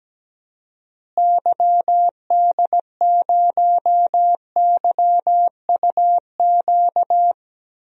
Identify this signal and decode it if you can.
Morse code, heard as a keyed tone, about 17 wpm, 700 Hz. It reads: YD0YUQ
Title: DXCC-100 Call Signs - 6 Characters - Repeated 2x - 17wpm